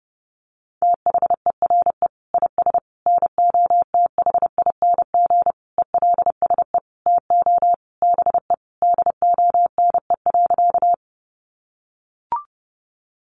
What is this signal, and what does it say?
Morse code, heard as a keyed tone, about 28 wpm, 700 Hz. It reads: THERE IS NOTHING ELSE TO BE DONE.
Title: Sentences from Top 800 Words 30wpm